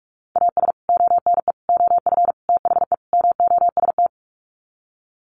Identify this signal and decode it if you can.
Morse code, heard as a keyed tone, about 44 wpm, 700 Hz. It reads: AS ONE OF THE MOST
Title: Sets of 5 Words 45wpm